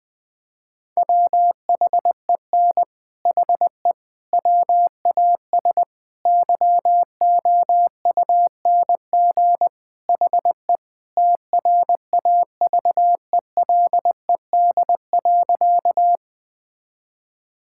Morse code, heard as a keyed tone, about 20 wpm, 700 Hz. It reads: WHEN HE WAS YOUNG HE TRAVELED.
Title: Sentences with 3-Word Phrases - 20wpm